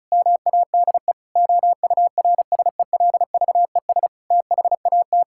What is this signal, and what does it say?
Morse code, heard as a keyed tone, about 36 wpm, 700 Hz. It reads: MADE OURSELVES THAT
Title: Sentences from Top 1600 Words 35wpm